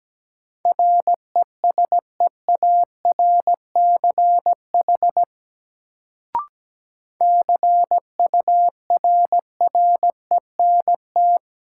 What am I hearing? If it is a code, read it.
Morse code, heard as a keyed tone, about 17 wpm, 700 Hz. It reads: RESEARCH CURRENT
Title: Sentences from Top 1000 Words - Review Part 4 - New Word Drill - 17wpm